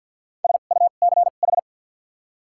Morse code, heard as a keyed tone, about 54 wpm, 700 Hz. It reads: S U C H
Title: 2x Character Spacing - Top 300 Words - 50wpm